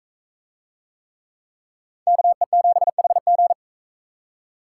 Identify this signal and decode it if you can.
Morse code, heard as a keyed tone, about 42 wpm, 700 Hz. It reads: KE7HG